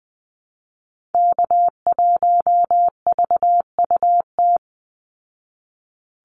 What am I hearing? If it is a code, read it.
Morse code, heard as a keyed tone, about 20 wpm, 700 Hz. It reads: K1VUT
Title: Call Signs Repeated 2x - 20wpm